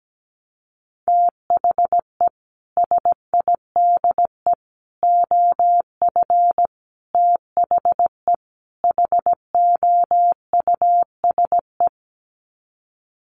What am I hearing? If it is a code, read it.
Morse code, heard as a keyed tone, about 17 wpm, 700 Hz. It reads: THE SIDE OF THE HOUSE
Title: Sets of 5 Words Encore2 17wpm